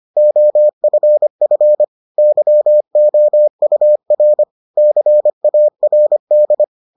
Morse code, heard as a keyed tone, about 25 wpm, 600 Hz. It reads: OFF YOUR CARD